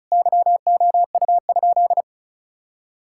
Morse code, heard as a keyed tone, about 34 wpm, 700 Hz. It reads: YOU?
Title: Sentences from Top 1000 Words - Review Part 5 - 35wpm